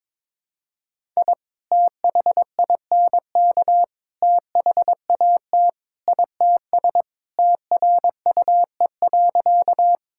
Morse code, heard as a keyed tone, about 22 wpm, 700 Hz. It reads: I THINK THAT ITS TRUE.
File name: practice / 22wpm / Sentences with 2-Word Phrases - 22wpm